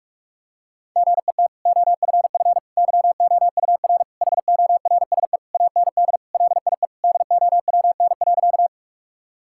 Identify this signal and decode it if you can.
Morse code, heard as a keyed tone, about 46 wpm, 700 Hz. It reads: GET OFF YOUR HORSE AND LIE DOWN.
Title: Sentences from Top 1600 Words 45wpm